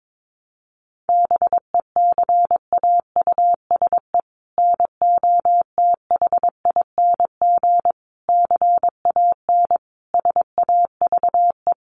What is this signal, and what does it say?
Morse code, heard as a keyed tone, about 22 wpm, 700 Hz. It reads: BECAUSE NOTHING CAN SAVE
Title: Sentences from Top 1100 Words 22wpm